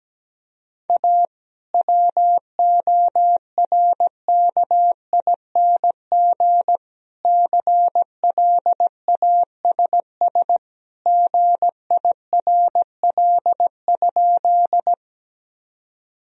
Morse code, heard as a keyed tone, about 17 wpm, 700 Hz. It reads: A WORKING CLASS GIRL?
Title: Sentences from Top 1000 Words - Review Part 4 - 17wpm